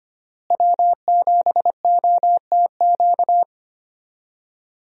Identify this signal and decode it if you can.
Morse code, heard as a keyed tone, about 25 wpm, 690 Hz. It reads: W7OTQ
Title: USA Call Signs - 5 Characters - Repeated 2x - 25wpm